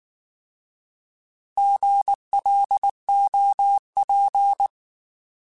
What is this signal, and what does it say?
Morse code, heard as a keyed tone, about 19 wpm, 800 Hz. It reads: GLOP